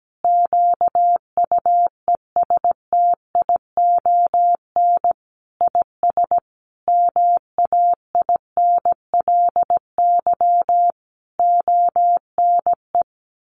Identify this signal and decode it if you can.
Morse code, heard as a keyed tone, about 17 wpm, 700 Hz. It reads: QUESTION IS MAINLY ONE
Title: Sentences from Top 1400 Words 17wpm